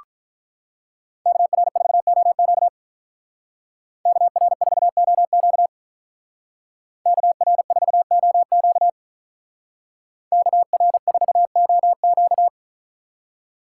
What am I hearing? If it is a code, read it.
Morse code, heard as a keyed tone, about 43 wpm, 700 Hz. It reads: KR4OQ KR4OQ KR4OQ KR4OQ
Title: Speed Racing - USA 2x2 Call Signs 35wpm